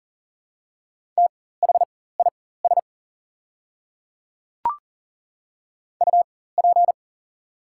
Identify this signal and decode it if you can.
Morse code, heard as a keyed tone, about 39 wpm, spaced as for 19 wpm, 700 Hz. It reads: THIS UP
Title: 4x Character Spacing - Top 300 Words - 40wpm